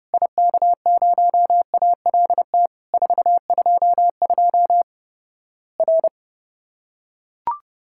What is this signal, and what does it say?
Morse code, heard as a keyed tone, about 30 wpm, 700 Hz. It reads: IK0ALT 422 R
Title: Worked All Europe DX Contest - QTC Exchange - 30wpm